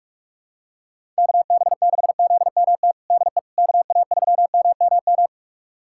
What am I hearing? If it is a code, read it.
Morse code, heard as a keyed tone, about 46 wpm, 700 Hz. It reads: KB6ZMT DE KA3MMM
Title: USA Callsign DE Callsign 45wpm